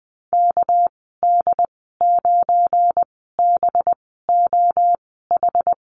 Morse code, heard as a keyed tone, about 20 wpm, 700 Hz. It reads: K D 9 B O H
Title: 238 Lesson 38g - Letter q - Call Signs 20wpm